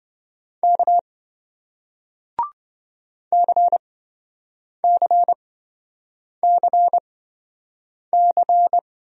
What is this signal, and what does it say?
Morse code, heard as a keyed tone, about 27 wpm, spaced as for 7 wpm, 700 Hz. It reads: K CCCC